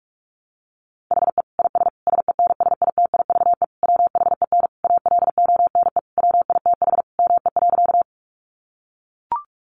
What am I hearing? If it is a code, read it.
Morse code, heard as a keyed tone, about 44 wpm, 700 Hz. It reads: HE IS SENSITIVE WHEN ALONE WITH ME.
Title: Sentences from Top 1400 Words 45wpm